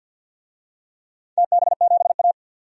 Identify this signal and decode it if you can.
Morse code, heard as a keyed tone, about 53 wpm, 700 Hz. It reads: TB7A
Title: DXCC-100 Call Signs - 4 Characters - 50wpm